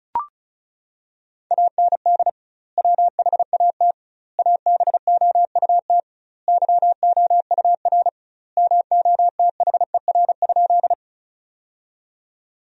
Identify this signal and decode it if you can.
Morse code, heard as a keyed tone, about 36 wpm, 700 Hz. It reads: AND WHAT ABOUT YOUR MOTHER?
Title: 2x Word-Spacing - Sentences from Top 200 Words - 35wpm